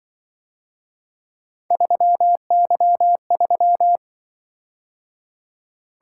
Morse code, heard as a keyed tone, about 23 wpm, 700 Hz. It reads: 3Y3
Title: Speed-Racing - Three Letter-Number Combinations - 20wpm